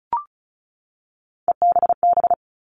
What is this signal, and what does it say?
Morse code, heard as a keyed tone, about 34 wpm, 700 Hz. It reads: EBB
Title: All Cut Numbers - 3-Digit 35wpm